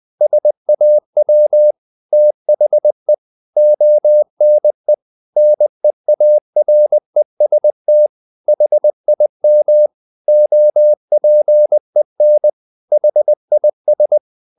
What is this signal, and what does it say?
Morse code, heard as a keyed tone, about 20 wpm, 600 Hz. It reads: SAW THE ONE NEAREST HIM OPEN HIS
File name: the old man and the sea part 2 20wpm